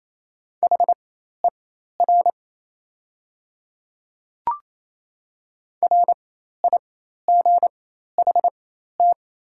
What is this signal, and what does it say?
Morse code, heard as a keyed tone, about 27 wpm, spaced as for 13 wpm, 700 Hz. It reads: HER RIGHT
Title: 4x Character Spacing - Top 500 Words - 28wpm